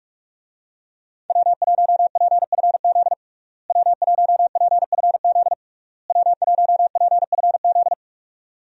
Morse code, heard as a keyed tone, about 46 wpm, 700 Hz. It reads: W1PFZ W1PFZ W1PFZ
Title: USA Call Signs - 5 Characters - Repeated 3x - 45wpm